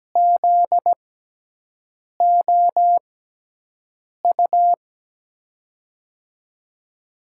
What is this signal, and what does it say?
Morse code, heard as a keyed tone, about 17 wpm, spaced as for 6 wpm, 700 Hz. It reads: ZOU
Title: USA Call Signs - 6 Characters - 6x Character Spacing - 17wpm